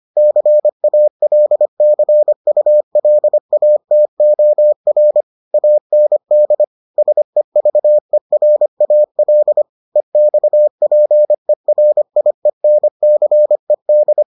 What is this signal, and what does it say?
Morse code, heard as a keyed tone, about 25 wpm, 600 Hz. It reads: CALCULATOR AND SEVERAL EXPERIENCED